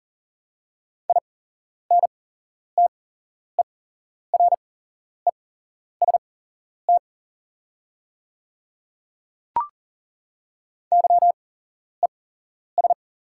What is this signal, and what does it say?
Morse code, heard as a keyed tone, about 40 wpm, spaced as for 11 wpm, 690 Hz. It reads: INTEREST YES
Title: 8x Character Spacing - Top 500 Words - 40wpm